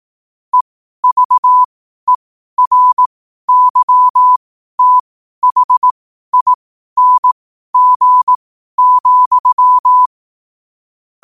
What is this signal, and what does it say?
Morse code, heard as a keyed tone, about 18 wpm, spaced as for 12 wpm, 1000 Hz. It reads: EVERYTHING,
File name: qotdfiles / 12wpm